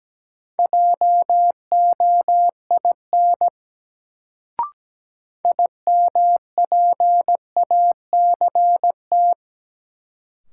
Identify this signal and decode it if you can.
Morse code, heard as a keyed tone, about 17 wpm, 700 Hz. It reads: JOIN IMPACT
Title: Sentences from Top 1000 Words - Review Part 6 - New Word Drill - 17wpm